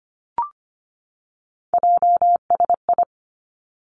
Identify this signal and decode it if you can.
Morse code, heard as a keyed tone, about 25 wpm, 700 Hz. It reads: JSI
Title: Three Challenging Letter-Number Combinations 25wpm